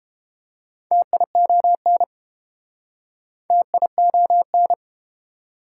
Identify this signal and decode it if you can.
Morse code, heard as a keyed tone, about 30 wpm, 700 Hz. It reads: TION TION